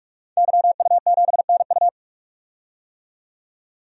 Morse code, heard as a keyed tone, about 47 wpm, 700 Hz. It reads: YU7NU
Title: DXCC-100 Call Signs - 5 Characters - 45wpm